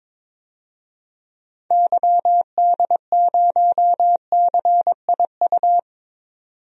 Morse code, heard as a keyed tone, about 21 wpm, 700 Hz. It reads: YD0CIU
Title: DXCC-100 Call Signs - 6 Characters - 22wpm